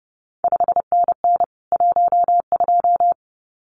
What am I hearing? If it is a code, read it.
Morse code, heard as a keyed tone, about 30 wpm, 700 Hz. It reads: 5NN 12
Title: CQ WW DX Contest Exchange Encore - 30wpm